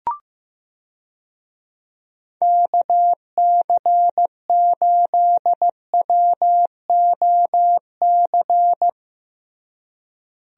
Morse code, heard as a keyed tone, about 14 wpm, 700 Hz. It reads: KC8WOC
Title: USA Call Signs - 6 Characters - Repeated 2x - 15wpm